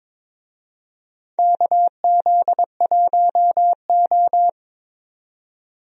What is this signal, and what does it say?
Morse code, heard as a keyed tone, about 22 wpm, 700 Hz. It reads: KZ1O
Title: USA Call Signs - 4 Characters - 22wpm